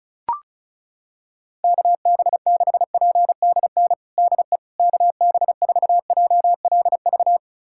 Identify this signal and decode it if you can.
Morse code, heard as a keyed tone, about 34 wpm, 700 Hz. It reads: KB6PDN DE KB4JLV
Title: USA Callsign DE Callsign - 6 Characters - 35wpm